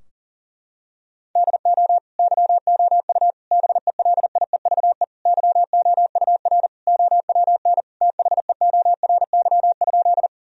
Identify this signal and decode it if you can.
Morse code, heard as a keyed tone, about 40 wpm, 700 Hz. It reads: DO YOU BELIEVE YOUR OWN THEORY?
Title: Sentences from Top 1000 Words - Review Part 3 - 40wpm